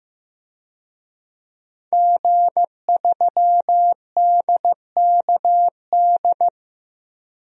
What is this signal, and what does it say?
Morse code, heard as a keyed tone, about 16 wpm, 700 Hz. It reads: G3DKD
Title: DXCC-100 Call Signs - 5 Characters - 15wpm